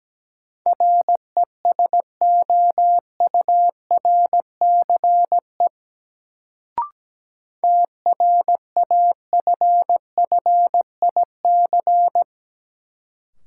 Morse code, heard as a keyed tone, about 17 wpm, 700 Hz. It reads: RESOURCE TRAFFIC
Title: Sentences from Top 1300 Words - New Word Drill - 17wpm